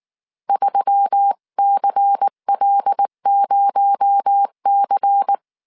Morse code, heard as a keyed tone, about 19 wpm, 800 Hz. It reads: 3CL0C
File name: CW / DET / 1 6 1500C S C 18wpm 2356